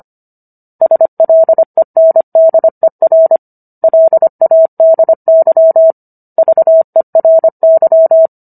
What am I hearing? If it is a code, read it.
Morse code, heard as a keyed tone, about 25 wpm, 650 Hz. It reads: SLENDER LADY VERY